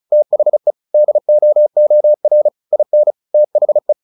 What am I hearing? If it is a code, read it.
Morse code, heard as a keyed tone, about 35 wpm, 600 Hz. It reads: THE DOOR IN THE
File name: the old man and the sea 2 35wpm